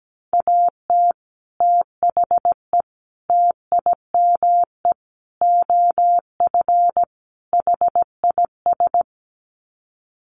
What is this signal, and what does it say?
Morse code, heard as a keyed tone, about 17 wpm, 700 Hz. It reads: AT THE TIME OF HIS
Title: Sets of 5 Words Encore2 17wpm